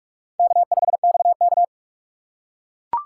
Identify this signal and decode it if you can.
Morse code, heard as a keyed tone, about 42 wpm, 700 Hz. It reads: K5XK
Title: K1USN SST QSO Practice - 45wpm